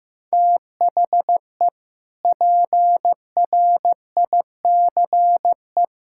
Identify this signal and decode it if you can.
Morse code, heard as a keyed tone, about 16 wpm, 700 Hz. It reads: THE PRICE
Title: Sentences from Top 1000 Words - Review Part 6 - 15wpm